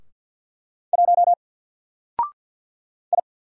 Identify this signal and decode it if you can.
Morse code, heard as a keyed tone, about 50 wpm, 700 Hz. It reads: 1 I
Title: Single Letter-Number 50wpm